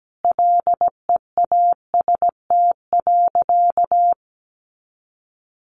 Morse code, heard as a keyed tone, about 17 wpm, 700 Hz. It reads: LEAST.
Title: Sentences from Top 700 Words 17wpm